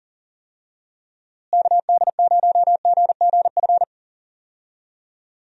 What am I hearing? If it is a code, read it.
Morse code, heard as a keyed tone, about 40 wpm, 700 Hz. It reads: KD0GGF
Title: USA Call Signs - 6 Characters - Repeated 2x - 40wpm